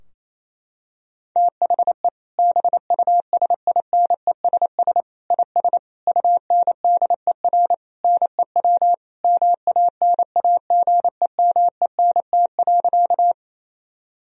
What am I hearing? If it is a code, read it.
Morse code, heard as a keyed tone, about 27 wpm, 700 Hz. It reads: THE BUSINESS IS UNDER NEW MANAGEMENT.
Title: Sentences from Top 1000 Words - Review Part 4 - 28wpm